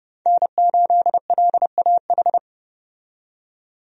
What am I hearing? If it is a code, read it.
Morse code, heard as a keyed tone, about 30 wpm, 700 Hz. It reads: N8LAH
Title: USA Call Signs - 5 Characters - 30wpm